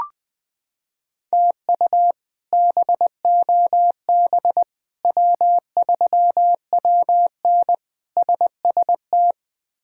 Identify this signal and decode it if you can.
Morse code, heard as a keyed tone, about 20 wpm, 700 Hz. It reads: TU BOB W3WN SST
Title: K1USN SST QSO Practice - 20wpm